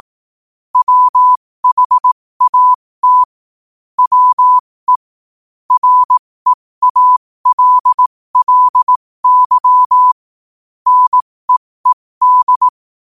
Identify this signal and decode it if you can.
Morse code, heard as a keyed tone, about 18 wpm, spaced as for 15 wpm, 1000 Hz. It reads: WHAT WE REALLY NEED
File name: qotdfiles / 15wpm